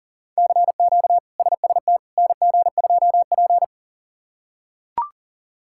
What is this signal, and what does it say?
Morse code, heard as a keyed tone, about 40 wpm, 700 Hz. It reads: CQ SST NG2P
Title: K1USN SST QSO Practice - 40wpm